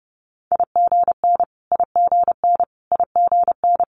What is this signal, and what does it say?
Morse code, heard as a keyed tone, about 31 wpm, 700 Hz. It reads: IGN IGN IGN
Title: Multi-letter Phonemes Repeated 3x 30wpm